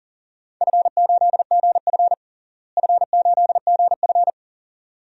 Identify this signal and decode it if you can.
Morse code, heard as a keyed tone, about 40 wpm, 700 Hz. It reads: F8GF F8GF
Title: DXCC-100 Call Signs - 4 Characters - Repeated 2x - 40wpm